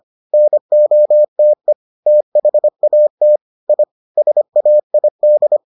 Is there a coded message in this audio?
Yes: Morse code from 0.3 to 5.6 s, heard as a keyed tone, about 25 wpm, 600 Hz. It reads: NOTE THAT I SAID